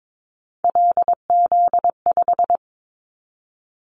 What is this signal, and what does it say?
Morse code, heard as a keyed tone, about 22 wpm, 700 Hz. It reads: LZ5
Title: Callsign Prefixes for Top 100 DXCC - 22wpm